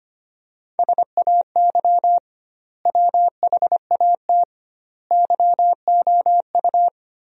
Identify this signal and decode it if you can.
Morse code, heard as a keyed tone, about 25 wpm, 700 Hz. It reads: SAY WHAT YOU
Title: 2x Word-Spacing - Sentences from Top 200 Words - 25wpm